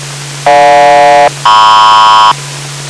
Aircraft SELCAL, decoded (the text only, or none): GJ-MP